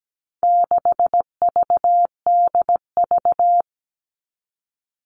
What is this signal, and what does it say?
Morse code, heard as a keyed tone, about 17 wpm, 700 Hz. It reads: 6VDV
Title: All Cut Numbers - 4-Digit 17wpm